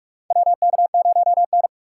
Morse code, heard as a keyed tone, about 44 wpm, 700 Hz. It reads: WK0N
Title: USA Call Signs - 4 Characters - 6x Character Spacing - 45wpm